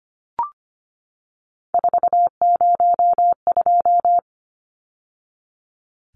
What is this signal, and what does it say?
Morse code, heard as a keyed tone, about 25 wpm, 700 Hz. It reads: 402
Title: Numbers - 3-Digit 25wpm